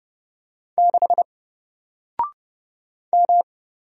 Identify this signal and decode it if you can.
Morse code, heard as a keyed tone, about 30 wpm, 700 Hz. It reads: 6 M
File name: Single-Letter-Number-30wpm